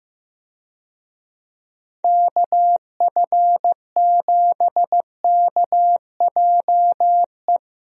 Morse code, heard as a keyed tone, about 15 wpm, 700 Hz. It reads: KF7KJE
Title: USA Call Signs - 6 Characters - Repeated 2x - 15wpm